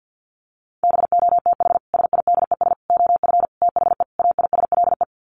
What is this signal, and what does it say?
Morse code, heard as a keyed tone, about 50 wpm, 700 Hz. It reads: BOTH SIDES OF THE AISLE
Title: Sets of 5 Words Encore2 50wpm